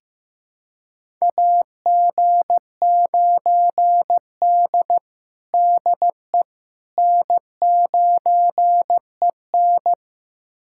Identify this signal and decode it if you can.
Morse code, heard as a keyed tone, about 15 wpm, 700 Hz. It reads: AG9D DE N9EN
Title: USA Callsign DE Callsign - 4 Characters - 15wpm